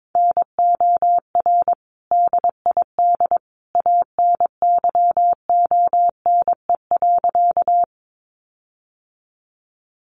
Morse code, heard as a keyed tone, about 22 wpm, 700 Hz. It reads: NOR DID ANYONE.
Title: Sentences from Top 1500 Words 22wpm